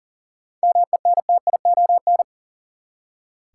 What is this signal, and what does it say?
Morse code, heard as a keyed tone, about 40 wpm, 700 Hz. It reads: MENTION
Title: Seven Letter Words - 40wpm